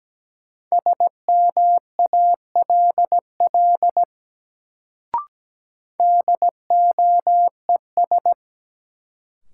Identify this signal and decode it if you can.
Morse code, heard as a keyed tone, about 17 wpm, 700 Hz. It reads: SMALL DOES
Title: Sentences from Top 200 Words - New Word Drill - 17wpm